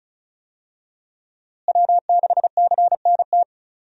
Morse code, heard as a keyed tone, about 34 wpm, 700 Hz. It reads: W6CNT
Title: USA Call Signs - 5 Characters - Repeated 2x - 35wpm